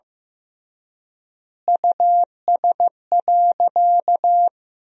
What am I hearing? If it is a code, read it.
Morse code, heard as a keyed tone, about 15 wpm, 700 Hz. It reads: US.
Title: 3x Word-Spacing - Sentences from Top 500 Words - 15wpm